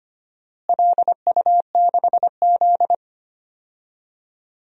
Morse code, heard as a keyed tone, about 25 wpm, 700 Hz. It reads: LU6Z
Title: DXCC-100 Call Signs - 4 Characters - Repeated 2x - 25wpm